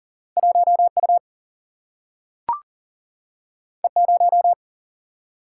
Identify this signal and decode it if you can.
Morse code, heard as a keyed tone, about 40 wpm, 700 Hz. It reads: 1 U E 0
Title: Single-Letter Single-Number 40wpm